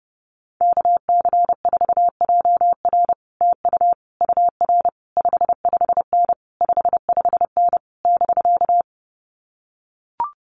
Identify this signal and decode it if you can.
Morse code, heard as a keyed tone, about 30 wpm, 700 Hz. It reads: KC4JR TU UR 55N 55N <BK>
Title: Common QSO Phrases Encore 30wpm